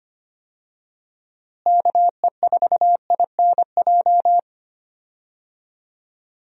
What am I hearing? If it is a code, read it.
Morse code, heard as a keyed tone, about 25 wpm, 700 Hz. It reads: KE4INJ
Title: USA Call Signs - 6 Characters - Repeated 3x - 25wpm